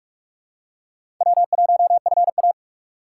Morse code, heard as a keyed tone, about 46 wpm, 700 Hz. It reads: W1FA
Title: USA Call Signs - 4 Characters - 3x Character Spacing - 45wpm